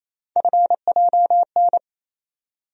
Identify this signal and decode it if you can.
Morse code, heard as a keyed tone, about 28 wpm, 700 Hz. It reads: FJN
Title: USA Call Sign Postfixes - 3 Letters 28wpm